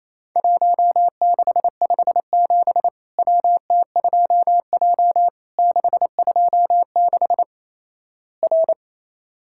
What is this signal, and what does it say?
Morse code, heard as a keyed tone, about 28 wpm, 700 Hz. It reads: 1657 WT2J 626 R
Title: Worked All Europe DX Contest - QTC Exchange - 28wpm